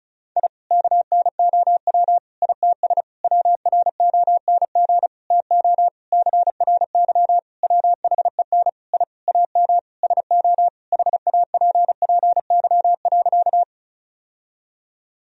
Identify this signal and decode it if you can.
Morse code, heard as a keyed tone, about 34 wpm, 700 Hz. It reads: I KNOW ITS WRONG TO CRY WHEN I AM SO HAPPY.
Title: Sentences from Top 1000 Words - Review Part 2 - 35wpm